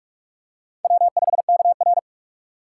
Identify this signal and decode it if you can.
Morse code, heard as a keyed tone, about 44 wpm, 700 Hz. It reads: W5KR